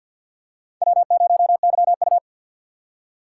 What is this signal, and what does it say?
Morse code, heard as a keyed tone, about 50 wpm, 700 Hz. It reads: W0YU
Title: USA Call Signs - 4 Characters - 2x Character Spacing - 50wpm